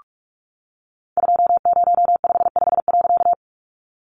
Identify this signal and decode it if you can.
Morse code, heard as a keyed tone, about 44 wpm, 700 Hz. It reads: 2055.
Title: A Sound Of Thunder 45wpm